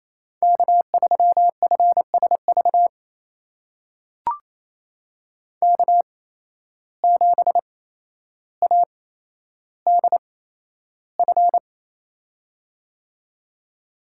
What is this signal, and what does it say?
Morse code, heard as a keyed tone, about 28 wpm, spaced as for 12 wpm, 700 Hz. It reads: K3FSV K7ADF
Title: USA Call Signs - 5 Characters - 8x Character Spacing - 28wpm